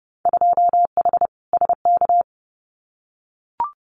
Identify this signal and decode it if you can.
Morse code, heard as a keyed tone, about 30 wpm, 700 Hz. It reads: 2H SK
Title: Winter Field Day Exchanges Encore 30wpm